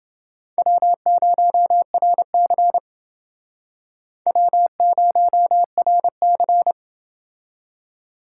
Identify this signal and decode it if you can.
Morse code, heard as a keyed tone, about 29 wpm, 700 Hz. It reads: W0RC W0RC